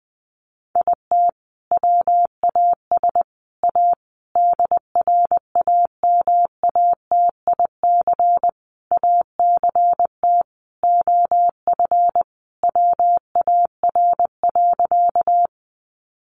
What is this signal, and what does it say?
Morse code, heard as a keyed tone, about 20 wpm, 700 Hz. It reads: IT WAS A DRAMATIC ACT OF WAR.
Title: Sentences from Top 1300 Words 20wpm